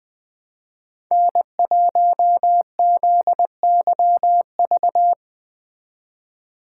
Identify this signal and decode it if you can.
Morse code, heard as a keyed tone, about 20 wpm, 700 Hz. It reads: N1ZYV
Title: Speed Racing - USA Call Signs 17wpm